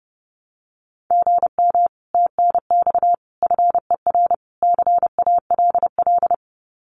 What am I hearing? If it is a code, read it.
Morse code, heard as a keyed tone, about 32 wpm, 700 Hz. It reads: GM TNX FER CALL
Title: Common QSO Phrases Encore 30wpm